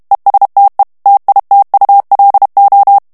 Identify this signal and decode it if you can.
Morse code, heard as a keyed tone, about 32 wpm, 800 Hz. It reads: ESTE TITULO
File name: morse mp3 / ESPAGNOL TEXTE EN CLAIR 2 VITESSE 30 MOTS MINUTES